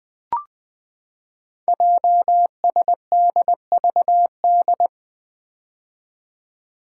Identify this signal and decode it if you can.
Morse code, heard as a keyed tone, about 20 wpm, 700 Hz. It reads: JSDVD
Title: Code Groups - 5 Letters - 20wpm